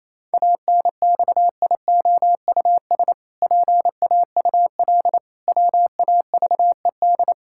Morse code, heard as a keyed tone, about 28 wpm, 700 Hz. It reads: ANXIOUS PAUL WAVED
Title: Pangram Challenge v2 - 28wpm